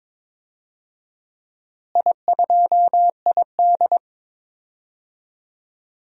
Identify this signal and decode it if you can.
Morse code, heard as a keyed tone, about 22 wpm, 700 Hz. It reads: I2ID
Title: DXCC-100 Call Signs - 4 Characters - Repeated 3x - 22wpm